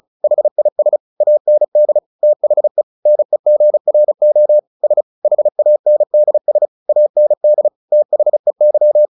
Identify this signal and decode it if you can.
Morse code, heard as a keyed tone, about 34 wpm, 600 Hz. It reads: HIS AND THE NEGRO S HANDS AND THEY
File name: the old man and the sea 2 35wpm